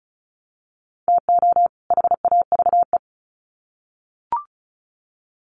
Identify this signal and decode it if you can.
Morse code, heard as a keyed tone, about 34 wpm, 700 Hz. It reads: TO HAVE
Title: Sets of 2 Words 35wpm